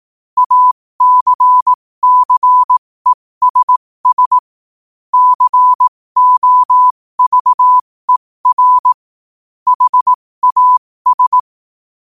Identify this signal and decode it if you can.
Morse code, heard as a keyed tone, about 18 wpm, spaced as for 15 wpm, 1000 Hz. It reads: ACCESS COVER HAS
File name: qotdfiles / 15wpm